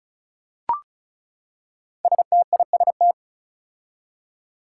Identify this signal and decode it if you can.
Morse code, heard as a keyed tone, about 36 wpm, 700 Hz. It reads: STIST
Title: Five Challenging Letter-Number Combinations 35wpm